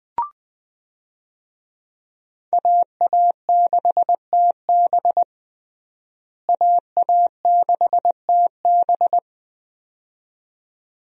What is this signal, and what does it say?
Morse code, heard as a keyed tone, about 20 wpm, 700 Hz. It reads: AA6TB AA6TB
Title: USA Call Signs - 5 Characters - Repeated 2x - 20wpm